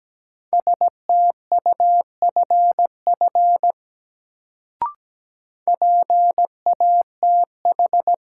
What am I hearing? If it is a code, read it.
Morse code, heard as a keyed tone, about 17 wpm, 700 Hz. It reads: STUFF PATH
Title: Sentences from Top 1400 Words - New Word Drill Alt - 17wpm